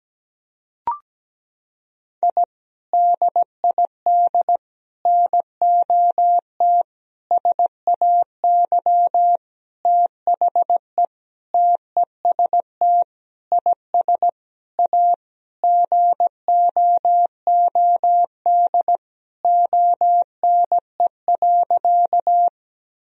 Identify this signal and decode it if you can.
Morse code, heard as a keyed tone, about 17 wpm, 700 Hz. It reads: I DID NOT SAY THE TEST IS A GOOD ONE.
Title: Sentences from Top 1000 Words - Review Part 3 - 17wpm